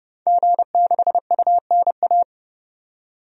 Morse code, heard as a keyed tone, about 30 wpm, 690 Hz. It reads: G6UNA
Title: DXCC-100 Call Signs - 5 Characters - 30wpm